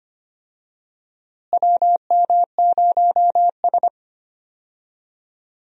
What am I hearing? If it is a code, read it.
Morse code, heard as a keyed tone, about 25 wpm, 700 Hz. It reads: WM0S
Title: USA Call Signs - 4 Characters - Repeated 3x - 25wpm